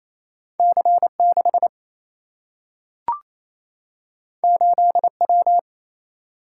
Morse code, heard as a keyed tone, about 28 wpm, 700 Hz. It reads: C 6 8 W
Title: Single-Letter Single-Number 28wpm